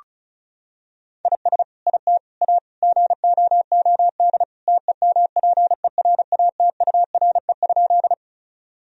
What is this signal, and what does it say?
Morse code, heard as a keyed tone, about 36 wpm, 700 Hz. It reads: IS IT A GOOD TEMPERATURE?
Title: Sentences from Top 1000 Words - Review Part 4 - 35wpm